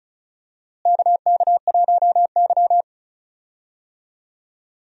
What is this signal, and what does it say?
Morse code, heard as a keyed tone, about 36 wpm, 700 Hz. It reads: KK1Y
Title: USA Call Signs - 4 Characters - Repeated 2x - 35wpm